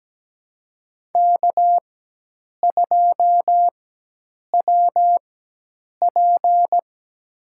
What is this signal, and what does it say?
Morse code, heard as a keyed tone, about 17 wpm, spaced as for 8 wpm, 700 Hz. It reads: K2WP
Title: USA Call Signs - 4 Characters - 4x Character Spacing - 17wpm